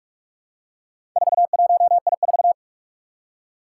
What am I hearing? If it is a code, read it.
Morse code, heard as a keyed tone, about 44 wpm, 700 Hz. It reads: 41I4
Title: Four Challenging Letter-Number Combinations 45wpm